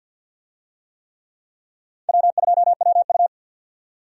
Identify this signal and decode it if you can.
Morse code, heard as a keyed tone, about 47 wpm, 700 Hz. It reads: W2WU
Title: USA Call Signs - 4 Characters - 50wpm